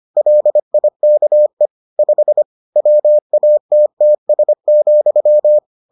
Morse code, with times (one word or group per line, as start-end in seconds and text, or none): 0.2-1.7 LIKE
2.0-2.4 5
2.8-5.6 WATTS,